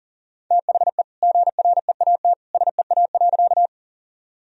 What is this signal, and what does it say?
Morse code, heard as a keyed tone, about 40 wpm, 700 Hz. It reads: THE GREAT SEA.
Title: Sentences from Top 1000 Words - Review Part 1 - 40wpm